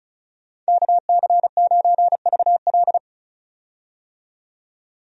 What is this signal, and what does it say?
Morse code, heard as a keyed tone, about 34 wpm, 700 Hz. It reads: KC9VL
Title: USA Call Signs - 5 Characters - 35wpm